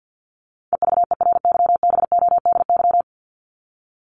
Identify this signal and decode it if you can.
Morse code, heard as a keyed tone, about 50 wpm, 700 Hz. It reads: EVERYBODY
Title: Top 1251-1500 Words 50wpm